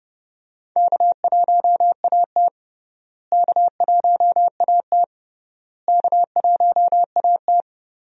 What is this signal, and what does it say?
Morse code, heard as a keyed tone, about 30 wpm, 700 Hz. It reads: K1AT K1AT K1AT
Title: USA Call Signs - 4 Characters - Repeated 3x - 30wpm